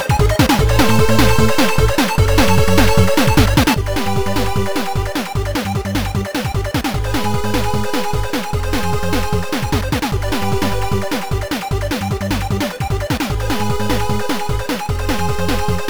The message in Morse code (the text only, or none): none